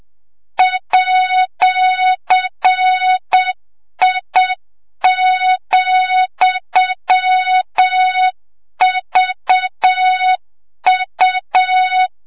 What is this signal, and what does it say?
Morse code, heard as a keyed tone, about 7 wpm, 3700 Hz, with noise. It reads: @I,VU
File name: Morze-HA2MN / hang / 25 GYAKORLAT